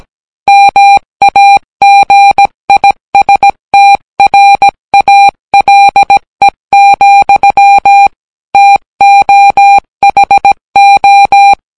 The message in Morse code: MAGISTRALE, TOHO